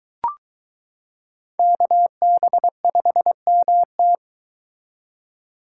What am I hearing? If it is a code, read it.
Morse code, heard as a keyed tone, about 22 wpm, 700 Hz. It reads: KB5MT